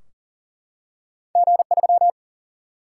Morse code, heard as a keyed tone, about 40 wpm, 700 Hz. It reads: G3
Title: Single-Letter Single-Number 40wpm